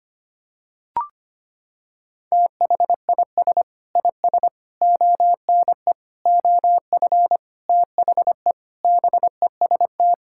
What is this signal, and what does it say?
Morse code, heard as a keyed tone, about 25 wpm, 700 Hz. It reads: THIS IS ONE OF THE BEST
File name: practice / 25wpm / Sentences with 4-Word Phrases - 25wpm